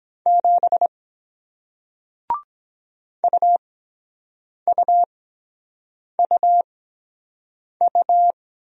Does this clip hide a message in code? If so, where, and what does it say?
Yes: Morse code from 0.3 to 8.3 s, heard as a keyed tone, about 23 wpm, spaced as for 7 wpm, 700 Hz. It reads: 7 UUUU